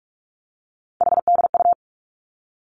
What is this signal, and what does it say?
Morse code, heard as a keyed tone, about 46 wpm, 700 Hz. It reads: HDU
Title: Three Challenging Letter-Number Combinations 45wpm